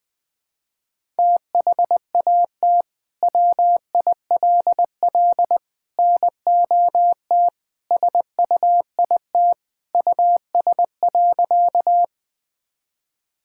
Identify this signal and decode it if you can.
Morse code, heard as a keyed tone, about 20 wpm, 700 Hz. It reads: THAT WILL NOT SUIT US.
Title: Sentences from Top 1600 Words 20wpm